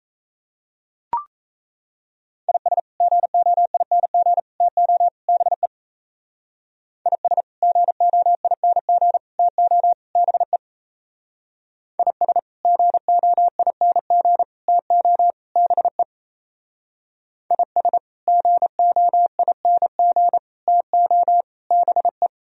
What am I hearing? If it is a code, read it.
Morse code, heard as a keyed tone, about 33 wpm, 700 Hz. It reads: IS GOING TO BE IS GOING TO BE IS GOING TO BE IS GOING TO BE
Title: Speed-Racing - Sets of 4 Words 28wpm